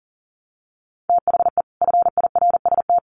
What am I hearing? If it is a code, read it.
Morse code, heard as a keyed tone, about 40 wpm, 700 Hz. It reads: THE FIRST
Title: Sets of 2 Words 40wpm